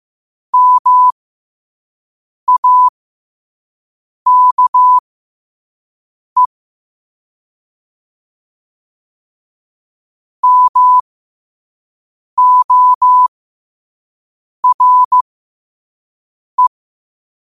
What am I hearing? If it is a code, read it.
Morse code, heard as a keyed tone, about 15 wpm, spaced as for 5 wpm, 1000 Hz. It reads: MAKE MORE
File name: qotdfiles / 5wpm